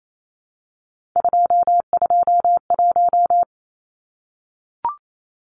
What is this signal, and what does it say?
Morse code, heard as a keyed tone, about 28 wpm, 700 Hz. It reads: 221
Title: Numbers - 3-Digit 28wpm